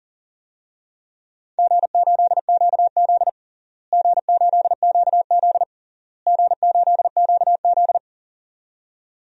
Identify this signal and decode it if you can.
Morse code, heard as a keyed tone, about 37 wpm, 700 Hz. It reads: G8QZ G8QZ G8QZ
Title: DXCC-100 Call Signs - 4 Characters - Repeated 3x - 40wpm